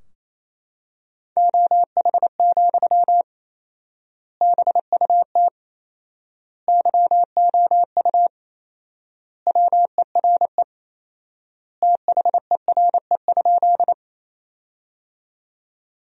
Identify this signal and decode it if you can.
Morse code, heard as a keyed tone, about 28 wpm, 700 Hz. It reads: OH, BUT YOU WERE THERE?
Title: 4x Word-Spacing - Sentences from Top 500 Words - 28wpm